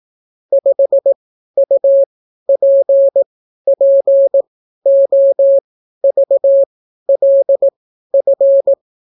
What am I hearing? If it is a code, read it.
Morse code, heard as a keyed tone, about 18 wpm, spaced as for 12 wpm, 550 Hz. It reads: 5UPPOVLF